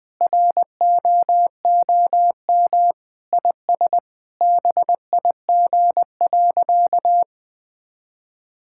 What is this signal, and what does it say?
Morse code, heard as a keyed tone, about 19 wpm, 700 Hz. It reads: ROOM IS BIG.
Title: Sentences with 3-Word Phrases - 20wpm